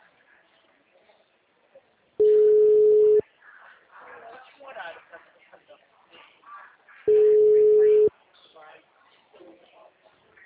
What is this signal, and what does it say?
Ringback tone (Europe, 425 Hz), with speech.
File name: out- -2003- - - .89912